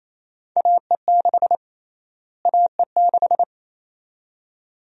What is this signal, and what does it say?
Morse code, heard as a keyed tone, about 28 wpm, 700 Hz. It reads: AE6 AE6
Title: USA Call Sign Prefixes - Repeated 2x - 28wpm